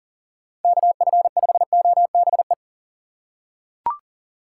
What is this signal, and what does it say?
Morse code, heard as a keyed tone, about 40 wpm, 700 Hz. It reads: KF5OBE
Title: USA Call Signs - 6 Characters - 4x Character Spacing - 40wpm